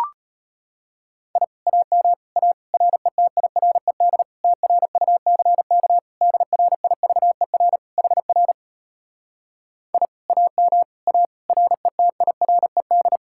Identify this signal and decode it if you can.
Morse code, heard as a keyed tone, about 33 wpm, 700 Hz. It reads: I AM A RETIRED TRUCK DRIVER HR I AM A RETIRED